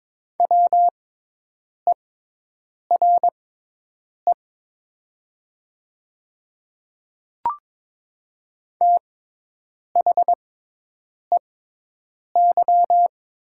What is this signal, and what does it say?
Morse code, heard as a keyed tone, about 22 wpm, spaced as for 8 wpm, 700 Hz. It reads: WERE THEY
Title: 6x Character Spacing - Top 100 Words - 22wpm